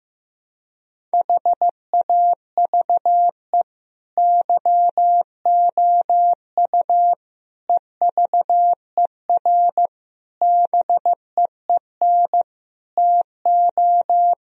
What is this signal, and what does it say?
Morse code, heard as a keyed tone, about 15 wpm, 700 Hz. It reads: HAVE YOU EVER BEEN TO
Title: Sentences from Top 1000 Words - Review Part 1 - 15wpm